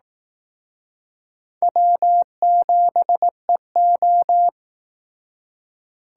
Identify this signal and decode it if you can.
Morse code, heard as a keyed tone, about 18 wpm, 700 Hz. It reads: W7EO
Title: Speed Racing - USA 1x2 Call Signs 15wpm